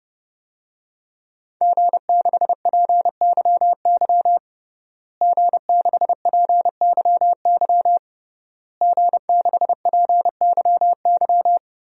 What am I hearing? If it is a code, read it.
Morse code, heard as a keyed tone, about 28 wpm, 700 Hz. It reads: G6PYY G6PYY G6PYY
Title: DXCC-100 Call Signs - 5 Characters - Repeated 3x - 30wpm